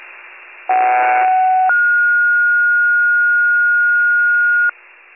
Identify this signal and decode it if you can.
Two-tone page, heard as a one-second tone, then a three-second tone, about 750 then 1430 Hz. It is heard from 0.7 to 4.7 s.